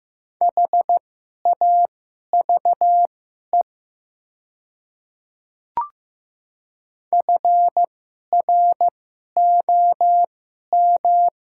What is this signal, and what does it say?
Morse code, heard as a keyed tone, about 15 wpm, spaced as for 11 wpm, 700 Hz. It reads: HAVE FROM
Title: 2x Character Spacing - Top 100 Words - 15wpm